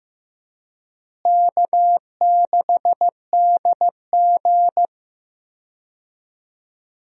Morse code, heard as a keyed tone, about 15 wpm, 700 Hz. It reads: K6DG
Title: USA Call Signs - 4 Characters - Repeated 3x - 15wpm